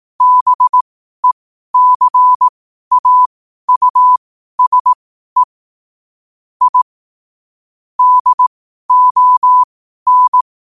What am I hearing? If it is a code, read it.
Morse code, heard as a keyed tone, about 18 wpm, spaced as for 12 wpm, 1000 Hz. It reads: BECAUSE I DON
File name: qotdfiles / 12wpm